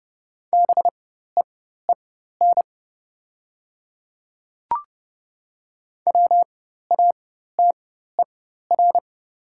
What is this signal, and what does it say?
Morse code, heard as a keyed tone, about 30 wpm, spaced as for 14 wpm, 690 Hz. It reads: BEEN WATER
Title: 4x Character Spacing - Top 500 Words - 30wpm